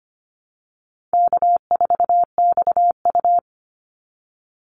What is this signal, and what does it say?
Morse code, heard as a keyed tone, about 25 wpm, 700 Hz. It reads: K4XU
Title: Easy Call Signs - 25wpm